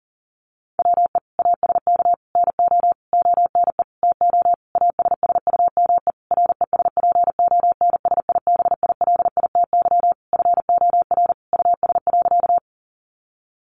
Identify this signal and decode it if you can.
Morse code, heard as a keyed tone, about 40 wpm, 700 Hz. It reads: WE ASK NO ONE TO ASSUME RESPONSIBILITY FOR US.
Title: Sentences from Top 1500 Words 40wpm